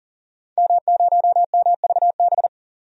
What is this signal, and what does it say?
Morse code, heard as a keyed tone, about 40 wpm, 700 Hz. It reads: M0MVB
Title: DXCC-100 Call Signs - 5 Characters - 40wpm